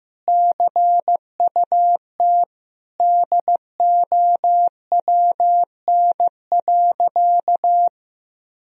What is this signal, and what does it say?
Morse code, heard as a keyed tone, about 15 wpm, 700 Hz. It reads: CUT DOWN.
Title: Sentences from Top 1000 Words - Review Part 1 - 15wpm